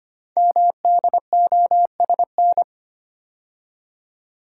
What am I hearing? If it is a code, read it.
Morse code, heard as a keyed tone, about 25 wpm, 700 Hz. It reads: MDOSN
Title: Code Groups - 5 Letter:Numbers - 25wpm